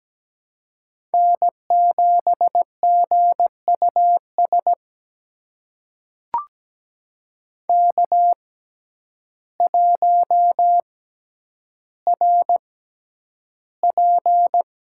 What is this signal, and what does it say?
Morse code, heard as a keyed tone, about 17 wpm, spaced as for 9 wpm, 700 Hz. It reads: N7GUS K1RP
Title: USA Call Signs - 5 Characters - 6x Character Spacing - 17wpm